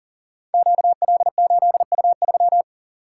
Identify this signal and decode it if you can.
Morse code, heard as a keyed tone, about 40 wpm, 700 Hz. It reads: QL8U3
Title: Code Groups - 5 Letter:Numbers - 40wpm